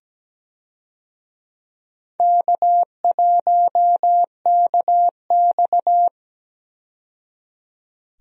Morse code, heard as a keyed tone, about 17 wpm, 700 Hz. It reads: K1KX
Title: USA Call Signs - 4 Characters - 17wpm